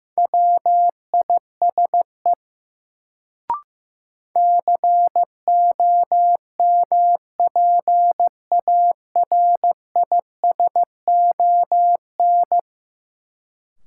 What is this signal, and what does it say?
Morse code, heard as a keyed tone, about 15 wpm, 700 Hz. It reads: WISE COMPARISON
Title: Sentences from Top 1600 Words - New Word Drill - 15wpm